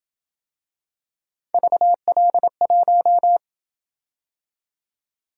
Morse code, heard as a keyed tone, about 26 wpm, 700 Hz. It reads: VL1